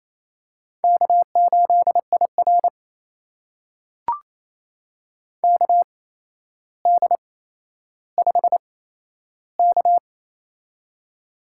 Morse code, heard as a keyed tone, about 27 wpm, spaced as for 12 wpm, 700 Hz. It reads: K8IR KD5K
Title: USA Call Signs - 4 Characters - 8x Character Spacing - 28wpm